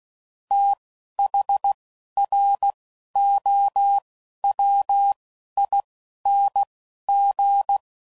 Morse code, heard as a keyed tone, about 16 wpm, spaced as for 11 wpm, 800 Hz. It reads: THROWING
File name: Prefixes / FLs & THs / Fast TH Prefixes